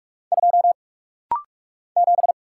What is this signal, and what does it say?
Morse code, heard as a keyed tone, about 45 wpm, 700 Hz. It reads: 2 7
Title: Single Letter-Number - Rapid-Fire 45wpm